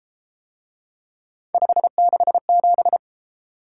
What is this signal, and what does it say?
Morse code, heard as a keyed tone, about 33 wpm, 700 Hz. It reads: 567